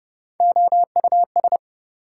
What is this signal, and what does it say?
Morse code, heard as a keyed tone, about 30 wpm, 700 Hz. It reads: OUS
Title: Prefixes and Suffixes 30wpm